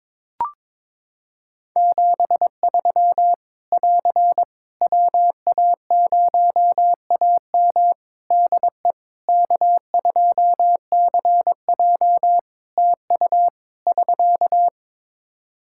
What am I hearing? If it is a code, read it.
Morse code, heard as a keyed tone, about 22 wpm, 700 Hz. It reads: 73 <AR> WA0AM DE K2CJ TU <SK>
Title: K1USN SST QSO Practice 22wpm